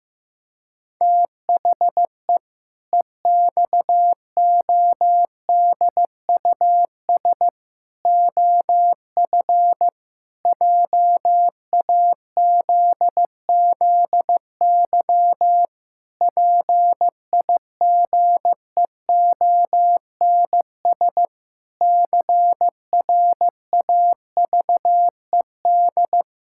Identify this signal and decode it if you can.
Morse code, heard as a keyed tone, about 15 wpm, 700 Hz. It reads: THE EXODUS OF JAZZY PIGEONS CRAVED
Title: Pangram Challenge v2 - 15wpm